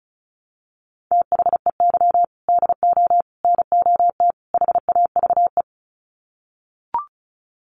Morse code, heard as a keyed tone, about 34 wpm, 700 Hz. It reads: THEY DO NOT HAVE
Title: Sets of 4 Words 35wpm